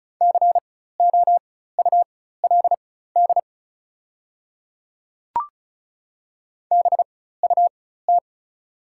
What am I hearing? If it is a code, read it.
Morse code, heard as a keyed tone, about 36 wpm, spaced as for 16 wpm, 700 Hz. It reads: COULD BUT
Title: 4x Character Spacing - Top 100 Words - 35wpm